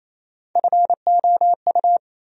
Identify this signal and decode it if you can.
Morse code, heard as a keyed tone, about 28 wpm, 700 Hz. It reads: FOU
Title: USA Call Sign Postfixes - 3 Letters 28wpm